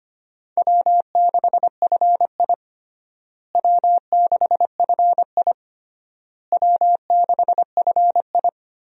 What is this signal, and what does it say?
Morse code, heard as a keyed tone, about 25 wpm, 700 Hz. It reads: W6FI W6FI W6FI
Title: USA Call Signs - 4 Characters - Repeated 3x - 25wpm